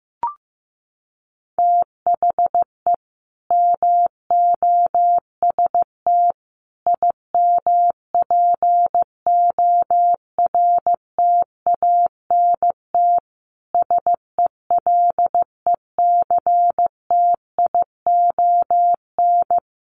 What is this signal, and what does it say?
Morse code, heard as a keyed tone, about 15 wpm, 700 Hz. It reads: THE MOST IMPORTANT SELECTION
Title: Sentences from Top 1400 Words 15wpm